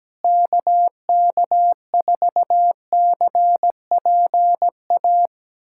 Morse code, heard as a keyed tone, about 17 wpm, 700 Hz. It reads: KK4CPA
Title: USA Callsign DE Callsign 17wpm